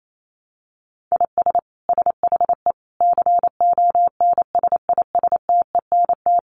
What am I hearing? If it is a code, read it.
Morse code, heard as a keyed tone, about 28 wpm, 700 Hz. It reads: IS SHE CONSISTENT
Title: Sentences from Top 1500 Words 28wpm